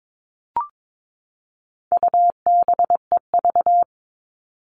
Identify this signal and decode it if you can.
Morse code, heard as a keyed tone, about 22 wpm, 700 Hz. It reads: UBEV
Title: All Cut Numbers - 4-Digit 22wpm